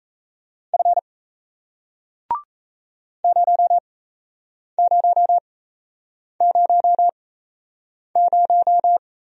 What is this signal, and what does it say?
Morse code, heard as a keyed tone, about 37 wpm, spaced as for 8 wpm, 700 Hz. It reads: F 0000